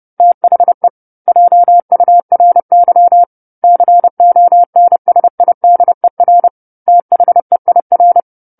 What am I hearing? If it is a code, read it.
Morse code, heard as a keyed tone, about 30 wpm, 700 Hz. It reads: THE JURY CONSIDER THEIR
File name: 10-Chapter-10 30 WPM